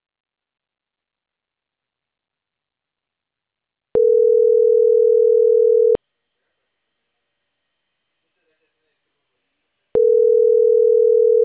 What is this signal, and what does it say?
Ringback tone (North America), 440 + 480 Hz, with speech.
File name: CENTRAL CAEA / out agcaea01